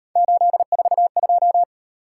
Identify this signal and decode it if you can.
Morse code, heard as a keyed tone, about 38 wpm, 700 Hz. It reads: 842